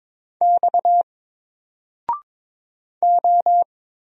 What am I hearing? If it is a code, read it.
Morse code, heard as a keyed tone, about 22 wpm, 700 Hz. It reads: X O
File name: Single-Letter-Number-22wpm